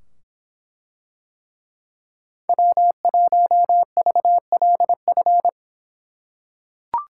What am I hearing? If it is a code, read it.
Morse code, heard as a keyed tone, about 24 wpm, 700 Hz. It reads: W1VLF